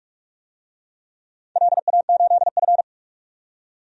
Morse code, heard as a keyed tone, about 44 wpm, 690 Hz. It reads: LA8F
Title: DXCC-100 Call Signs - 4 Characters - 45wpm